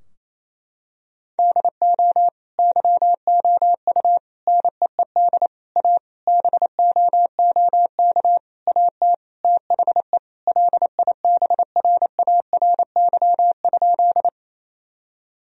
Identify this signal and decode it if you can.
Morse code, heard as a keyed tone, about 28 wpm, 700 Hz. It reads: DO YOU NEED A BOOK AT THE LIBRARY?
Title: Sentences from Top 1000 Words - Review Part 4 - 28wpm